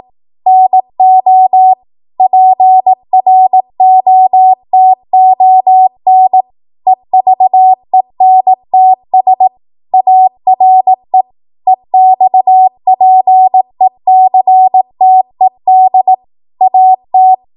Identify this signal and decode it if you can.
Morse code, heard as a keyed tone, about 18 wpm, 750 Hz. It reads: NO PROTON EVENTS ARE EXPECTED AT
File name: Morse / 18WPM